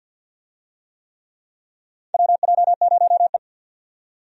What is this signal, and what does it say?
Morse code, heard as a keyed tone, about 53 wpm, 700 Hz. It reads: WJ0E
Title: USA Call Signs - 4 Characters - 50wpm